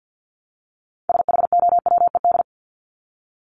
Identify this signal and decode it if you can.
Morse code, heard as a keyed tone, about 50 wpm, 700 Hz. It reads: SHOWED
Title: ED Suffix - Top 100 Words with Base Words 50wpm